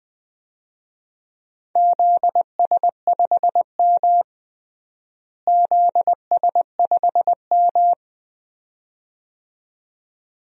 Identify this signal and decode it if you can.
Morse code, heard as a keyed tone, about 21 wpm, 700 Hz. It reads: ZS5M ZS5M
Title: DXCC-100 Call Signs - 4 Characters - Repeated 2x - 20wpm